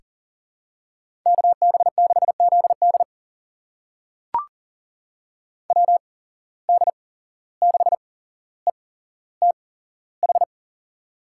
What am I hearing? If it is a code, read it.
Morse code, heard as a keyed tone, about 40 wpm, spaced as for 16 wpm, 700 Hz. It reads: KB6ZD WD6ETH
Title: USA Call Signs - 8x Character Spacing - 40wpm